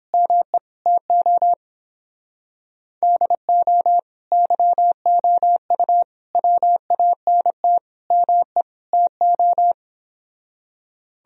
Morse code, heard as a keyed tone, about 29 wpm, 700 Hz. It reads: ME TO DO YOU WANT ME TO